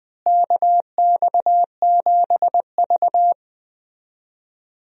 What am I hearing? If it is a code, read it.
Morse code, heard as a keyed tone, about 20 wpm, 700 Hz. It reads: KX7V
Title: USA Call Signs - 4 Characters - Repeated 3x - 20wpm